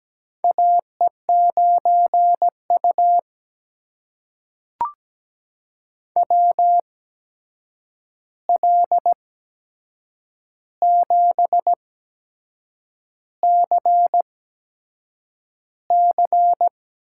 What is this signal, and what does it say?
Morse code, heard as a keyed tone, about 17 wpm, spaced as for 7 wpm, 700 Hz. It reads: AE9U WL7CC
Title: USA Call Signs - 8x Character Spacing - 17wpm